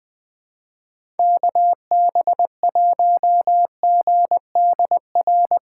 Morse code, heard as a keyed tone, about 20 wpm, 700 Hz. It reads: KB1GDR
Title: USA Call Signs - 6 Characters - 3x Character Spacing - 20wpm